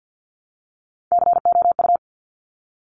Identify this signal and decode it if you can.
Morse code, heard as a keyed tone, about 50 wpm, 700 Hz. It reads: COU
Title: Top Three Letter Combinations in Words 50wpm